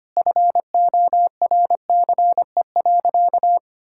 Morse code, heard as a keyed tone, about 25 wpm, 700 Hz. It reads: FORCE.
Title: Sentences from Top 1000 Words - Review Part 6 - 25wpm